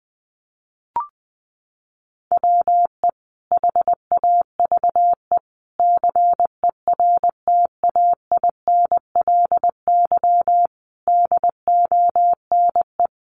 Morse code, heard as a keyed tone, about 20 wpm, 700 Hz. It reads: WE HAVE CERTAINLY DONE
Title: Sentences from Top 900 Words 20wpm